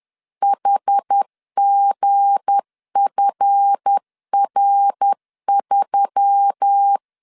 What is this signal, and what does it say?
Morse code, heard as a keyed tone, about 10 wpm, 800 Hz. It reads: HGFR3
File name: LEZIONI / 7 GF9Z / 7 GF9Z 10wpm 2611